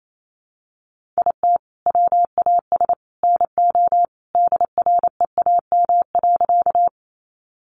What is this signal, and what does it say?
Morse code, heard as a keyed tone, about 28 wpm, 700 Hz. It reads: IT WAS NO DREAM.
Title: Sentences from Top 1500 Words 28wpm